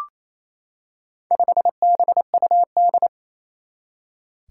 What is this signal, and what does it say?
Morse code, heard as a keyed tone, about 27 wpm, 700 Hz. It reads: 5BUD
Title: Four Challenging Letter-Number Combinations 28wpm